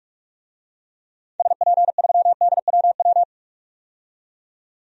Morse code, heard as a keyed tone, about 42 wpm, 690 Hz. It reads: SP3DWW
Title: DXCC-100 Call Signs - 6 Characters - 45wpm